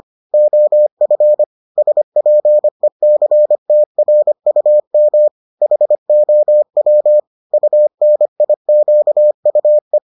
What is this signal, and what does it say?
Morse code, heard as a keyed tone, about 25 wpm, 600 Hz. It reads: OF SPECTRUM HOW UNIQUE